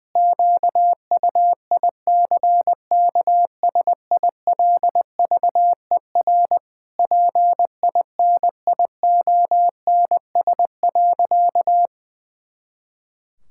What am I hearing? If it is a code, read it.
Morse code, heard as a keyed tone, about 20 wpm, 700 Hz. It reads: QUICKSILVER PINIONS.
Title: Pangram Challenge v2 - 20wpm